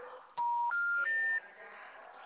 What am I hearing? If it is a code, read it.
Special information tone, with noise.